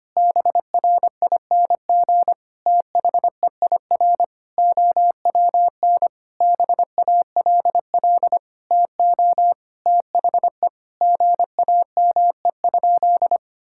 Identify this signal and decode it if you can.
Morse code, heard as a keyed tone, about 25 wpm, 700 Hz. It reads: BRING THEIR OWN BALL TO THE GAME?
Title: Sentences from Top 1000 Words - Review Part 6 - 25wpm